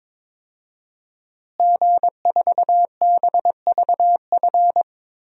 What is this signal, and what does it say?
Morse code, heard as a keyed tone, about 22 wpm, 700 Hz. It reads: G4BVF
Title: DXCC-100 Call Signs - 5 Characters - Repeated 3x - 22wpm